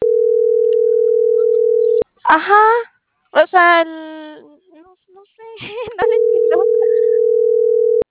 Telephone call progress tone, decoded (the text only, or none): ringback tone